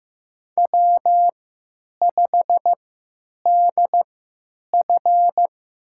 Morse code, heard as a keyed tone, about 15 wpm, spaced as for 8 wpm, 700 Hz. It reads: W5DF
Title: USA Call Signs - 4 Characters - 3x Character Spacing - 15wpm